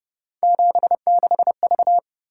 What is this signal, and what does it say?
Morse code, heard as a keyed tone, about 28 wpm, 700 Hz. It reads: 76V